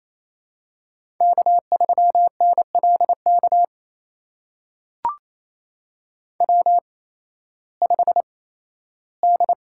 Morse code, heard as a keyed tone, about 28 wpm, spaced as for 15 wpm, 700 Hz. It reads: K3NLK W5D
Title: USA Call Signs - 5 Characters - 8x Character Spacing - 28wpm